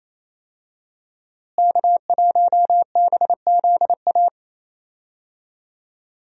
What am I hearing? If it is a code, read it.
Morse code, heard as a keyed tone, about 28 wpm, 700 Hz. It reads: K1BZA
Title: USA Call Signs - 5 Characters - Repeated 3x - 28wpm